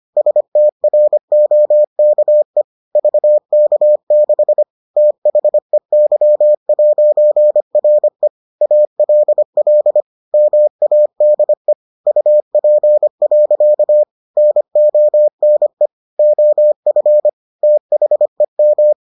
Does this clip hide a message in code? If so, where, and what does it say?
Yes: Morse code from 0.2 to 18.9 s, heard as a keyed tone, about 25 wpm, 600 Hz. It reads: STROKE VK6 THEY'RE ALL MADE UP. NONE OF THEM